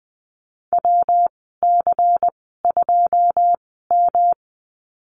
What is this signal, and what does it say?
Morse code, heard as a keyed tone, about 20 wpm, spaced as for 15 wpm, 700 Hz. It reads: WC2M
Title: 167 Lesson 27g - Number 7 - Call Signs 20wpm